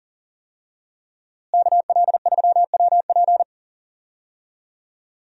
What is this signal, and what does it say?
Morse code, heard as a keyed tone, about 40 wpm, 700 Hz. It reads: KL3WP
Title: USA Call Signs - 5 Characters - Repeated 2x - 40wpm